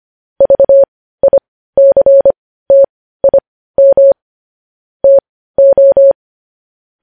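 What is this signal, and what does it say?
Morse code, heard as a keyed tone, about 25 wpm, spaced as for 15 wpm, 550 Hz. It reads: VICTIM TO
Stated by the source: Morse news headlines practice sent at 15 WPM.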